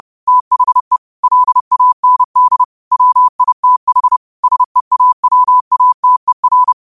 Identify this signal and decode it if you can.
Morse code, heard as a keyed tone, about 30 wpm, 1000 Hz. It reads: THE LAND WITH SEAWATER
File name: qotdfiles / 30wpm